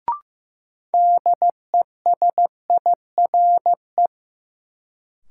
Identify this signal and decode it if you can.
Morse code, heard as a keyed tone, about 15 wpm, 700 Hz. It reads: DESIRE
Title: Sentences from Top 1100 Words - New Word Drill Alt - 15wpm